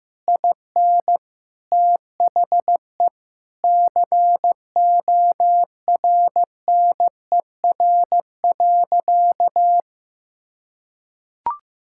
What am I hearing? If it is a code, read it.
Morse code, heard as a keyed tone, about 15 wpm, 700 Hz. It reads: IN THE CORNER.
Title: Sentences from Top 1600 Words 15wpm